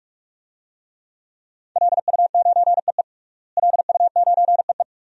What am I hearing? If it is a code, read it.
Morse code, heard as a keyed tone, about 46 wpm, 700 Hz. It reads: LU9EE LU9EE
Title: DXCC-100 Call Signs - 5 Characters - Repeated 2x - 45wpm